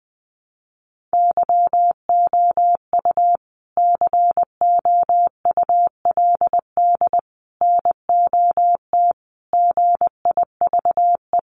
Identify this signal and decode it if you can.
Morse code, heard as a keyed tone, about 20 wpm, 700 Hz. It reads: YOU COULD NOT GIVE
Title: Sentences from Top 1500 Words 20wpm